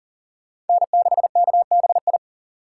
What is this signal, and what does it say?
Morse code, heard as a keyed tone, about 50 wpm, 700 Hz. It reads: N6KBI
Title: Speed Racing - USA 1x3 Call Signs 40wpm